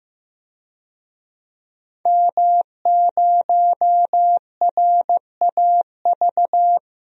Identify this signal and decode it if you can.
Morse code, heard as a keyed tone, about 15 wpm, 700 Hz. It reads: M0RAV
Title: DXCC-100 Call Signs - 5 Characters - 15wpm